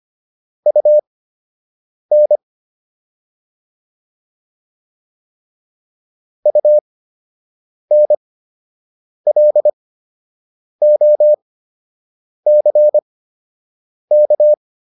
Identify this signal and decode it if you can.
Morse code, heard as a keyed tone, about 25 wpm, spaced as for 7 wpm, 600 Hz. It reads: UN UNLOCK